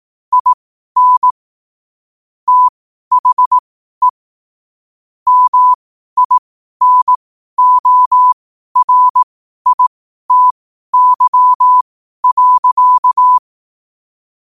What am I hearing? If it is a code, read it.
Morse code, heard as a keyed tone, about 18 wpm, spaced as for 12 wpm, 1000 Hz. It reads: IN THE MINORITY.
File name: 12wpm